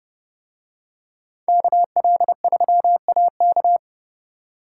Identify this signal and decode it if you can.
Morse code, heard as a keyed tone, about 28 wpm, 700 Hz. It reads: KL3AK